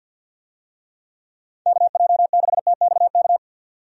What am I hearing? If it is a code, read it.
Morse code, heard as a keyed tone, about 51 wpm, 700 Hz. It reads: KJ6TXK
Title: USA Call Signs - 6 Characters - 50wpm